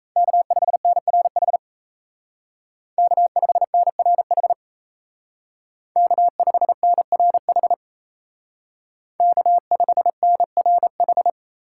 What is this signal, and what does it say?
Morse code, heard as a keyed tone, about 33 wpm, 700 Hz. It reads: K5NRH K5NRH K5NRH K5NRH
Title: Speed Racing - USA 1x3 Call Signs 28wpm